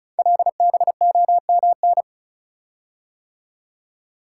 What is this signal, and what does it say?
Morse code, heard as a keyed tone, about 36 wpm, 690 Hz. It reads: LBOMN
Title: Code Groups - 5 Letters - 35wpm